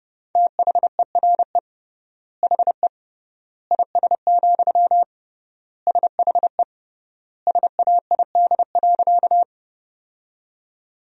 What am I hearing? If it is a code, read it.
Morse code, heard as a keyed tone, about 30 wpm, 700 Hz. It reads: THERE HE IS, SHE SAID.
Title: 3x Word-Spacing - Sentences from Top 100 Words - 30wpm